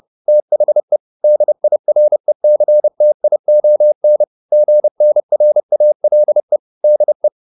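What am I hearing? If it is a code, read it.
Morse code, heard as a keyed tone, about 30 wpm, 600 Hz. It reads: THE DIRECTION GNRALE DE
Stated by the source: Morse code at 30 wpm, the novel Never by Ken Follett in English, from CW med Gnister.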